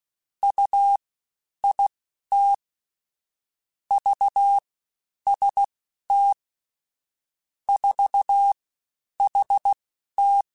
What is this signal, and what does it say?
Morse code, heard as a keyed tone, about 16 wpm, spaced as for 10 wpm, 800 Hz. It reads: UIT VST 4HT
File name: CW Comparisons / CW Comparisons-mp3s / CW Comparisons Broken Characters @15